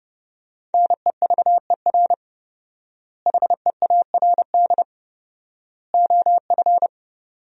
Morse code, heard as a keyed tone, about 30 wpm, 700 Hz. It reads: NEVER HEARD OF
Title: 4x Word-Spacing - Sentences from Top 400 Words - 30wpm